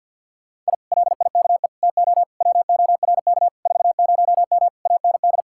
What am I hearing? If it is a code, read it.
Morse code, heard as a keyed tone, about 61 wpm, 700 Hz. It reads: I LIKE TO WORK 40M AND
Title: Speed-Racing Common QSO Phrases - 50wpm